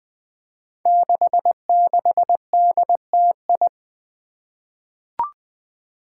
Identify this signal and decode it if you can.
Morse code, heard as a keyed tone, about 20 wpm, 700 Hz. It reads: 66DTI
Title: Five Challenging Letter-Number Combinations 20wpm